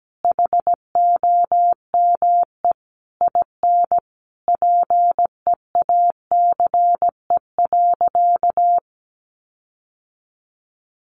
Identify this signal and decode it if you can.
Morse code, heard as a keyed tone, about 17 wpm, 700 Hz. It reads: HOME IN PEACE.
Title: Sentences from Top 1500 Words 17wpm